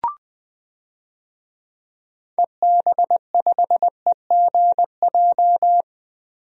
Morse code, heard as a keyed tone, about 20 wpm, 700 Hz. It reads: EB5EGJ
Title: DXCC-100 Call Signs - 6 Characters - Repeated 2x - 20wpm